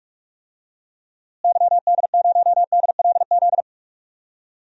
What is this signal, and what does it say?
Morse code, heard as a keyed tone, about 44 wpm, 700 Hz. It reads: YD0DLZ
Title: DXCC-100 Call Signs - 6 Characters - Repeated 3x - 45wpm